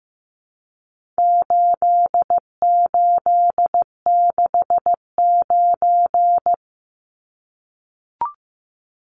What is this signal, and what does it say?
Morse code, heard as a keyed tone, about 15 wpm, 700 Hz. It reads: 8869
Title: Numbers - 4-Digit 15wpm